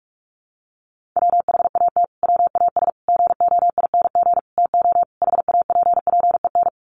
Morse code, heard as a keyed tone, about 44 wpm, 700 Hz. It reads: WHAT WAS GOING TO HAPPEN
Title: Sets of 5 Words Encore 45wpm